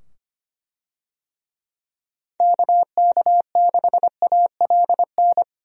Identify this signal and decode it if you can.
Morse code, heard as a keyed tone, about 25 wpm, 700 Hz. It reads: KK6ALN
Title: USA Call Signs - 6 Characters - 25wpm